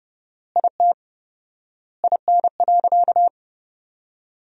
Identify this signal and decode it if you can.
Morse code, heard as a keyed tone, about 28 wpm, 700 Hz. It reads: IT IN.
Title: 4x Word-Spacing - Sentences from Top 200 Words - 30wpm